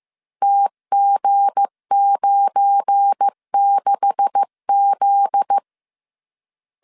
Morse code, heard as a keyed tone, about 15 wpm, 800 Hz. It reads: TG96Z
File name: CW / DET / 1 9 1500C S A 14wpm 3222